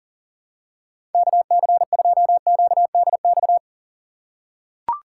Morse code, heard as a keyed tone, about 40 wpm, 690 Hz. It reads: KC2QDX
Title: USA Call Signs - 4x Character Spacing - 40wpm